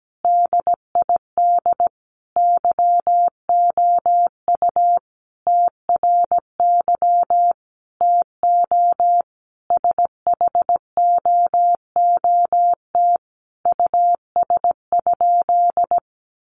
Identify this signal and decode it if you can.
Morse code, heard as a keyed tone, about 17 wpm, 700 Hz. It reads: DID YOU TRY TO SHOOT US?
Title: Sentences from Top 1200 Words 17wpm